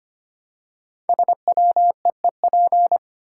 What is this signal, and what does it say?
Morse code, heard as a keyed tone, about 25 wpm, 700 Hz. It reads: SWEEP
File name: 5 Letter Words - First 3 Letters Given - 25wpm